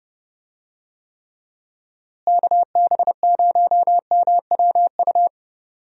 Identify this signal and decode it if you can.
Morse code, heard as a keyed tone, about 29 wpm, 700 Hz. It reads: KB0MWU
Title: USA Call Signs - 6 Characters - Repeated 3x - 30wpm